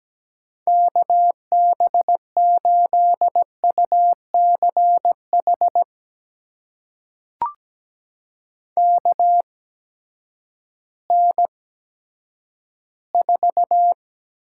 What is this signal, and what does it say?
Morse code, heard as a keyed tone, about 17 wpm, spaced as for 10 wpm, 700 Hz. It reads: KB8UCH KN4
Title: USA Call Signs - 6 Characters - 8x Character Spacing - 17wpm